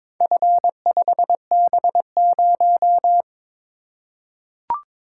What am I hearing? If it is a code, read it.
Morse code, heard as a keyed tone, about 23 wpm, 700 Hz. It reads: F5B0
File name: practice / 22wpm / Four Letter-Number Combinations - 22wpm